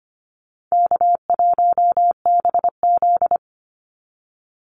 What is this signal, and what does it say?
Morse code, heard as a keyed tone, about 25 wpm, 700 Hz. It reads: K1BZ
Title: Call Signs Repeated 3x - 25wpm